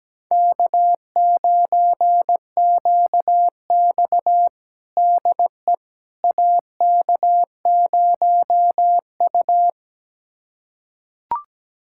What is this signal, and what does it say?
Morse code, heard as a keyed tone, about 17 wpm, 700 Hz. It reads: K9QX DE AK0U
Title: USA Callsign DE Callsign - 4 Characters - 17wpm